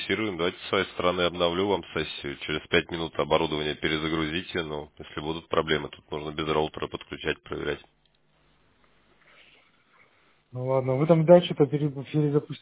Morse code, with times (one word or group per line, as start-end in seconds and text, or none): none